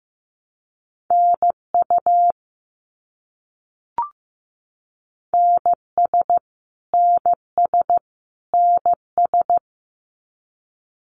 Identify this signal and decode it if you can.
Morse code, heard as a keyed tone, about 15 wpm, 700 Hz. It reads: NU NS NS NS
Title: Canadian Provinces and Territories - Repeated 3x - 15wpm